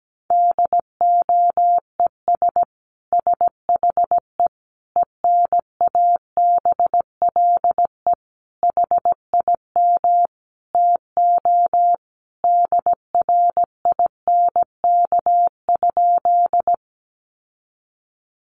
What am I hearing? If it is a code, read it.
Morse code, heard as a keyed tone, about 17 wpm, 700 Hz. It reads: DOES SHE ENABLE HIM TO DRINK?
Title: Sentences from Top 1500 Words 17wpm